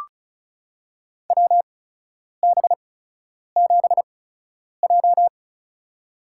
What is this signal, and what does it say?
Morse code, heard as a keyed tone, about 34 wpm, spaced as for 10 wpm, 700 Hz. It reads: WB7J
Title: USA Call Signs - 4 Characters - 8x Character Spacing - 35wpm